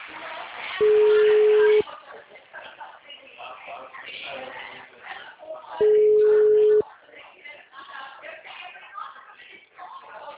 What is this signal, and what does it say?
Ringback tone (Europe, 425 Hz), with speech.
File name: out- -2009- - - .1344